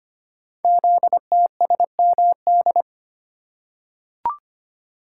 Morse code, heard as a keyed tone, about 25 wpm, 700 Hz. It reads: ZTSMD
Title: Code Groups - 5 Letters - 25wpm